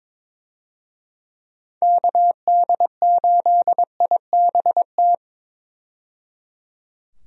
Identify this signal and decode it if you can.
Morse code, heard as a keyed tone, about 22 wpm, 700 Hz. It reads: KD8IBT
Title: USA Call Signs - 6 Characters - 22wpm